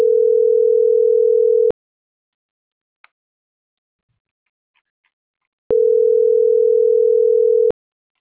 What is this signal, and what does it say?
Ringback tone (North America), 440 + 480 Hz.